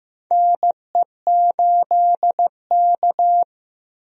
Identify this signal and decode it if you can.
Morse code, heard as a keyed tone, about 15 wpm, 700 Hz. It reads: NE8K
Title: USA Call Signs - 4 Characters - Repeated 3x - 15wpm